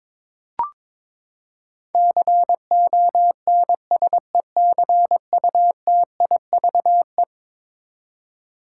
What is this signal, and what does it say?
Morse code, heard as a keyed tone, about 21 wpm, 700 Hz. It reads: CONSECUTIVE
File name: practice / 22wpm / Eleven Letter Words - 22wpm